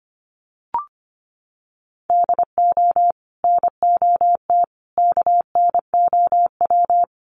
Morse code, heard as a keyed tone, about 25 wpm, 700 Hz. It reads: DO NOT KNOW
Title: Sets of 5 Words 25wpm